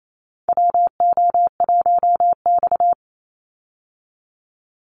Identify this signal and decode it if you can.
Morse code, heard as a keyed tone, about 28 wpm, 700 Hz. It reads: WO1X
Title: Call Signs Repeated 2x - 28wpm